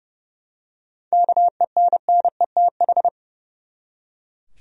Morse code, heard as a keyed tone, about 31 wpm, 700 Hz. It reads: KENNETH
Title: USA Common Names - 30wpm